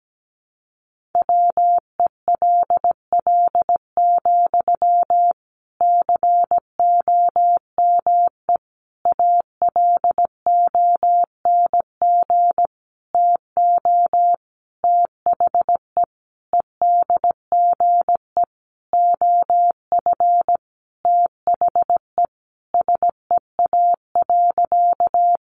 Morse code, heard as a keyed tone, about 17 wpm, 700 Hz. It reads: WELL, COME ALONG TO THE EDGE OF THE SEA.
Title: Sentences from Top 1300 Words 17wpm